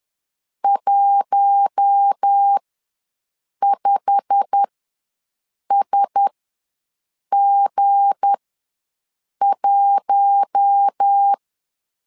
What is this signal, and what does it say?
Morse code, heard as a keyed tone, about 10 wpm, spaced as for 6 wpm, 800 Hz. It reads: 15SG1
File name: LEZIONI / 8 YJ45 / 8 YJ45 ws 10wpm 3843